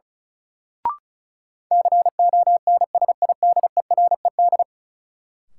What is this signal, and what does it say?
Morse code, heard as a keyed tone, about 36 wpm, 700 Hz. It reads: CONSIDERED
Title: Sets of 5 Words - New Word Drill - 35wpm